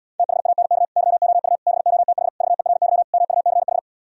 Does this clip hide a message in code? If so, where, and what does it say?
Yes: Morse code from 0.2 to 3.8 s, heard as a keyed tone, about 77 wpm, 700 Hz. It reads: THANK YOU BOTH VERY MUCH